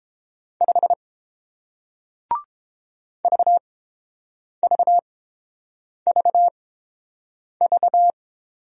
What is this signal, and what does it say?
Morse code, heard as a keyed tone, about 30 wpm, spaced as for 7 wpm, 700 Hz. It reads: 5 VVVV